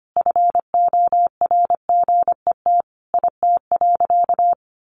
Morse code, heard as a keyed tone, about 25 wpm, 700 Hz. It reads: FORGET IT.
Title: Sentences from Top 1200 Words 25wpm